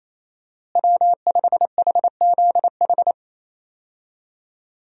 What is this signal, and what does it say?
Morse code, heard as a keyed tone, about 28 wpm, 700 Hz. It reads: W5HZH
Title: USA Call Signs - 5 Characters - Repeated 3x - 28wpm